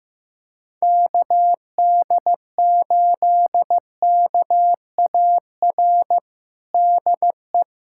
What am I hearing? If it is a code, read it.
Morse code, heard as a keyed tone, about 15 wpm, 700 Hz. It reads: KD8KAR DE
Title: USA Callsign DE Callsign - 6 Characters - 15wpm